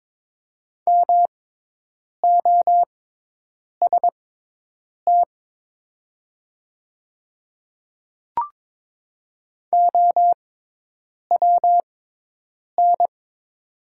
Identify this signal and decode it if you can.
Morse code, heard as a keyed tone, about 22 wpm, spaced as for 8 wpm, 700 Hz. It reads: MOST OWN
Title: 6x Character Spacing - Top 300 Words - 22wpm